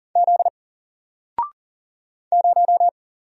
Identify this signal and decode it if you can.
Morse code, heard as a keyed tone, about 40 wpm, 700 Hz. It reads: Z 0
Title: Single Letter-Number 40wpm